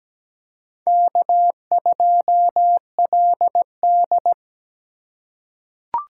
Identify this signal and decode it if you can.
Morse code, heard as a keyed tone, about 17 wpm, 700 Hz. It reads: K2LD
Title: USA Call Signs - 2x Character Spacing - 17wpm